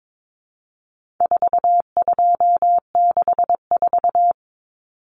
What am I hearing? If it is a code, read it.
Morse code, heard as a keyed tone, about 22 wpm, 700 Hz. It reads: 4264
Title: Common Cut Numbers - 4-Digit 22wpm